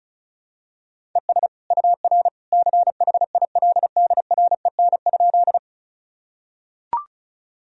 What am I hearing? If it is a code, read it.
Morse code, heard as a keyed tone, about 36 wpm, 700 Hz. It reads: ES UR CHILDREN?
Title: Sentences from Top 300 Words and Abbreviations 35wpm